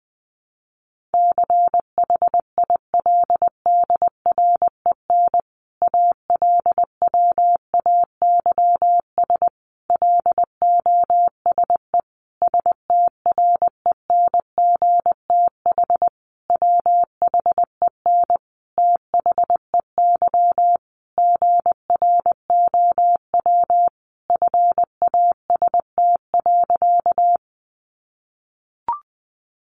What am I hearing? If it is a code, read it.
Morse code, heard as a keyed tone, about 20 wpm, 700 Hz. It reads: CHILDREN ALWAYS LOSE STRENGTH WHEN THEY GROW FAST.
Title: Sentences from Top 1300 Words 20wpm